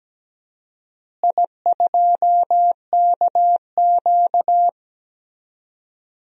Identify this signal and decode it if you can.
Morse code, heard as a keyed tone, about 17 wpm, 700 Hz. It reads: I2KQ
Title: DXCC-100 Call Signs - 4 Characters - Repeated 3x - 17wpm